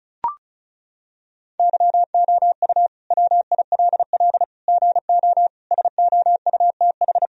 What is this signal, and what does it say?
Morse code, heard as a keyed tone, about 34 wpm, 700 Hz. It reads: YOU WILL GO SOUTH
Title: Sentences from Top 1000 Words - Review Part 2 - 35wpm